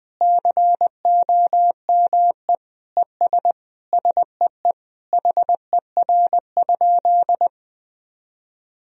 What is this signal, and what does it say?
Morse code, heard as a keyed tone, about 21 wpm, 700 Hz. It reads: COME ES SEE HER?
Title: Sentences from Top 100 Words and Abbreviations 20wpm